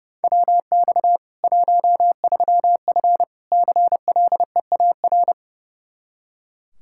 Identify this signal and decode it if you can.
Morse code, heard as a keyed tone, about 30 wpm, 700 Hz. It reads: WX 13F CLEAR
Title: K1USN SST QSO Practice 30wpm